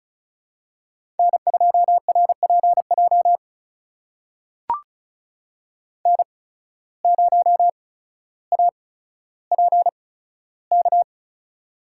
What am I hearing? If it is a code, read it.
Morse code, heard as a keyed tone, about 36 wpm, spaced as for 15 wpm, 690 Hz. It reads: N2RPJ N0APK
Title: USA Call Signs - 5 Characters - 8x Character Spacing - 35wpm